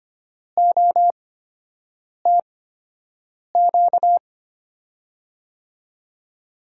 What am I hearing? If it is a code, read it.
Morse code, heard as a keyed tone, about 25 wpm, spaced as for 7 wpm, 700 Hz. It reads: OTQ
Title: USA Call Signs - 8x Character Spacing - 25wpm